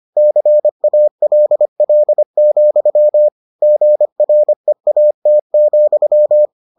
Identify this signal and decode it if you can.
Morse code, heard as a keyed tone, about 25 wpm, 600 Hz. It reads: CALL, GREAT,